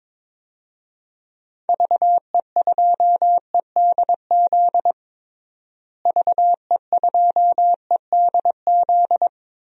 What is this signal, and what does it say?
Morse code, heard as a keyed tone, about 22 wpm, 700 Hz. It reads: VE2EDZ VE2EDZ
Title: DXCC-100 Call Signs - 6 Characters - Repeated 3x - 22wpm